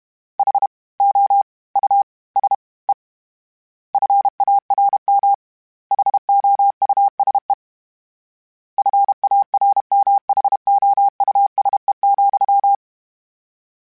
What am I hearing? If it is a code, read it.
Morse code, heard as a keyed tone, about 32 wpm, 800 Hz. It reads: H O U S E FARM HOUSE FARMHOUSE,
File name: General / General-mp3s / Houses @30